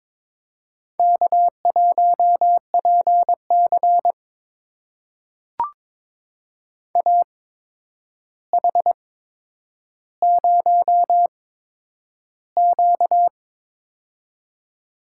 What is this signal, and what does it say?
Morse code, heard as a keyed tone, about 22 wpm, spaced as for 9 wpm, 700 Hz. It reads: K1PC AH0Q
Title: USA Call Signs - 8x Character Spacing - 22wpm